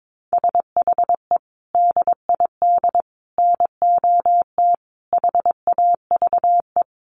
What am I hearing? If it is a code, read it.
Morse code, heard as a keyed tone, about 22 wpm, 700 Hz. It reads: SHE DID NOT HAVE
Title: Sentences from Top 1300 Words 22wpm